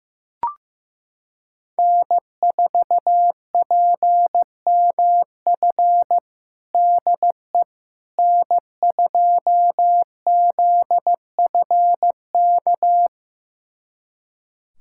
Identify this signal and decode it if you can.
Morse code, heard as a keyed tone, about 15 wpm, 700 Hz. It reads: N4PMF DE N2ZFK
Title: USA Callsign DE Callsign - 5 Characters - 15wpm